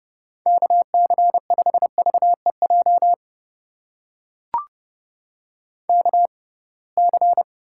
Morse code, heard as a keyed tone, about 30 wpm, spaced as for 23 wpm, 700 Hz. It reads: KC5VEJ KC
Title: USA Call Signs - 6 Characters - 6x Character Spacing - 30wpm